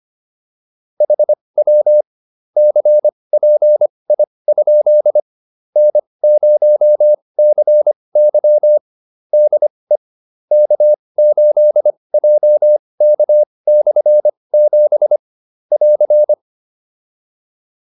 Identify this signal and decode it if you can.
Morse code, heard as a keyed tone, about 25 wpm, spaced as for 20 wpm, 600 Hz. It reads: HW CPI? N0CY DE K8JK/7 <AR>
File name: QSO310a 20